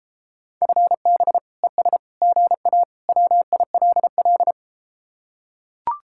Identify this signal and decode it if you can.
Morse code, heard as a keyed tone, about 32 wpm, 700 Hz. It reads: FB ES GA WILL